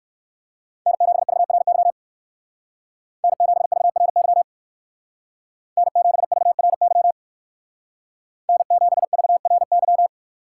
Morse code, heard as a keyed tone, about 58 wpm, 700 Hz. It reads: N7VRY N7VRY N7VRY N7VRY